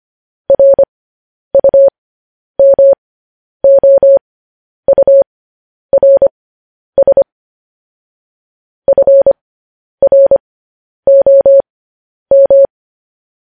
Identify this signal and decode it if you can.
Morse code, heard as a keyed tone, about 25 wpm, spaced as for 10 wpm, 550 Hz. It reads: RUMOURS FROM
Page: news headlines in morse code at 10 wpm